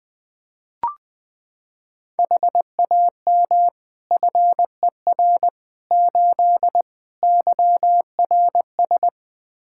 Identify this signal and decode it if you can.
Morse code, heard as a keyed tone, about 20 wpm, 700 Hz. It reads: HAM FER 8 YRS
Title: Practice QSO - Exchange 3 - More Info - 20wpm